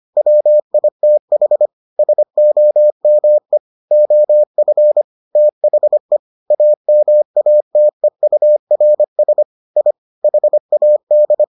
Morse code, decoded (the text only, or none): WITH SOME OF THE AMATEURS I HAD